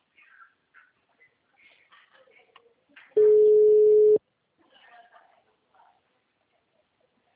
Ringback tone (Europe, 425 Hz).
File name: out- -2012- - - .63776